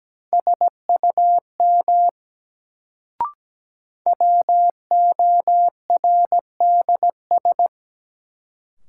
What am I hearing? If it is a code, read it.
Morse code, heard as a keyed tone, about 17 wpm, 700 Hz. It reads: SUM WORDS
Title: Sentences from Top 1000 Words - Review Part 6 - New Word Drill - 17wpm